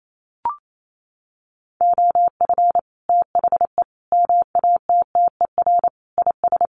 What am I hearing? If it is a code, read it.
Morse code, heard as a keyed tone, about 28 wpm, 700 Hz. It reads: OF THE MATTER IS
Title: Sets of 5 Words Encore 28wpm